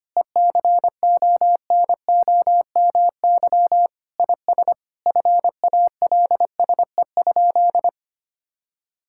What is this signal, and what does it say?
Morse code, heard as a keyed tone, about 25 wpm, 700 Hz. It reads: ECONOMY IS FALSE?
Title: Sentences from Top 1600 Words 25wpm